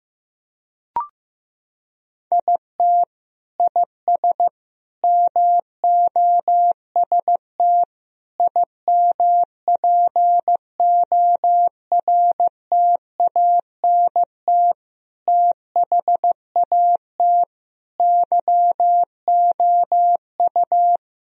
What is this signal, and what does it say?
Morse code, heard as a keyed tone, about 15 wpm, 700 Hz. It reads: IT IS MOST IMPORTANT THAT YOU
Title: Sentences from Top 1000 Words - Review Part 1 - 15wpm